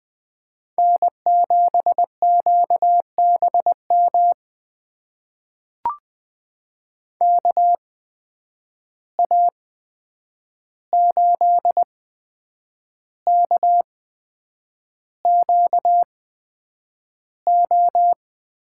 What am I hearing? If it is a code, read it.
Morse code, heard as a keyed tone, about 20 wpm, spaced as for 8 wpm, 700 Hz. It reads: N7QBM KA8KQO
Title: USA Call Signs - 8x Character Spacing - 20wpm